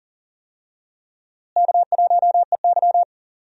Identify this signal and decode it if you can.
Morse code, heard as a keyed tone, about 39 wpm, 700 Hz. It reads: K1EY
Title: USA Call Signs - 4 Characters - Repeated 2x - 40wpm